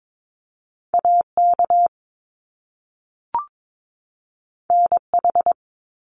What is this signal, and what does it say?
Morse code, heard as a keyed tone, about 22 wpm, 700 Hz. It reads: AK NH
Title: US States 22wpm